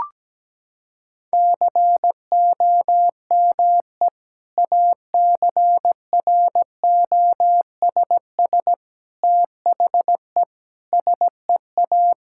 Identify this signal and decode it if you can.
Morse code, heard as a keyed tone, about 17 wpm, 700 Hz. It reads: COME ACROSS THE SEA
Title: Sentences from Top 1000 Words - Review Part 5 - 17wpm